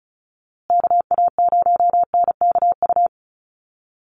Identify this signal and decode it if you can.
Morse code, heard as a keyed tone, about 34 wpm, 700 Hz. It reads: KA0NKU
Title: USA Call Signs Repeated 3x 35wpm